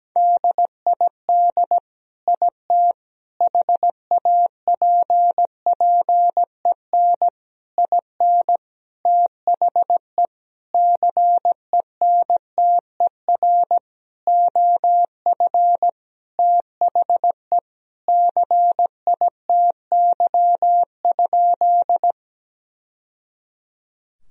Morse code, heard as a keyed tone, about 17 wpm, 700 Hz. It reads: DID IT HAPPEN IN THE CENTER OF THE CITY?
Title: Sentences from Top 1000 Words - Review Part 2 - 17wpm